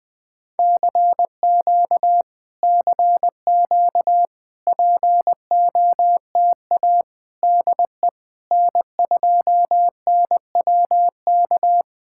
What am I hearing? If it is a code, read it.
Morse code, heard as a keyed tone, about 21 wpm, 700 Hz. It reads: CQ CQ POTA DE N2NWK
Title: POTA QSO Practice - 20wpm